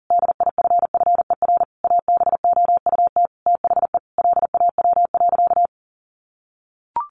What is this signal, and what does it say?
Morse code, heard as a keyed tone, about 40 wpm, 700 Hz. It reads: DIFFER ABOUT THE LAW.
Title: Sentences from Top 1400 Words 40wpm